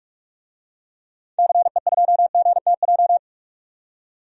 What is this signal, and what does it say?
Morse code, heard as a keyed tone, about 46 wpm, 700 Hz. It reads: CE2GTJ
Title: DXCC-100 Call Signs - 6 Characters - 45wpm